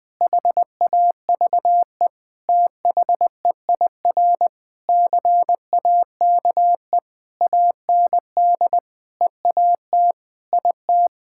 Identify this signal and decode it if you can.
Morse code, heard as a keyed tone, about 20 wpm, 700 Hz. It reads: HAVE THEIR CAKE AND EAT IT
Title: Sentences with 8-Word Phrases - 20wpm